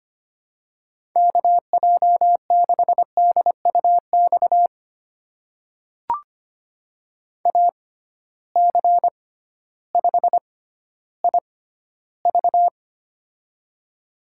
Morse code, heard as a keyed tone, about 25 wpm, spaced as for 13 wpm, 700 Hz. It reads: KJ6DUX AC5IV
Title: USA Call Signs - 6x Character Spacing - 25wpm